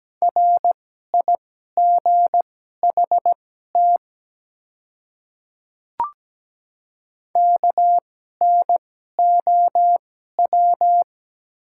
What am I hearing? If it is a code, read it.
Morse code, heard as a keyed tone, about 17 wpm, spaced as for 12 wpm, 700 Hz. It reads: RIGHT KNOW
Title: 2x Character Spacing - Top 200 Words - 17wpm